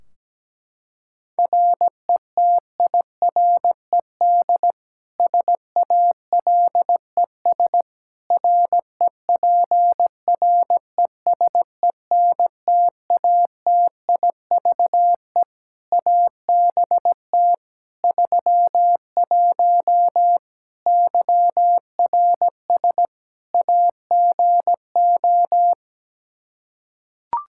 Morse code, heard as a keyed tone, about 17 wpm, 700 Hz. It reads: RETIRED SALES REPRESENTATIVE ABT 31 YRS AGO
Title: Practice QSO - Exchange 3 - More Info - 17wpm